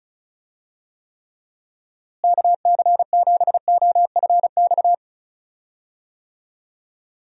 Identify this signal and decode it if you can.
Morse code, heard as a keyed tone, about 34 wpm, 700 Hz. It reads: KC7OFX
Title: USA Call Signs - 6 Characters - Repeated 3x - 35wpm